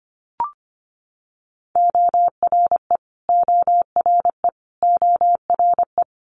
Morse code, heard as a keyed tone, about 25 wpm, 700 Hz. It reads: ORE ORE ORE
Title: Multi-letter Phonemes Repeated 3x 25wpm